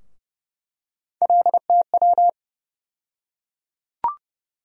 Morse code, heard as a keyed tone, about 32 wpm, 700 Hz. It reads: LTW
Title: USA Call Sign Postfixes - 3 Letters 30wpm